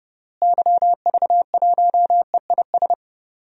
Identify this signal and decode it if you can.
Morse code, heard as a keyed tone, about 30 wpm, 700 Hz. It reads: YV1EIS
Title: DXCC-100 Call Signs - 6 Characters - Repeated 3x - 30wpm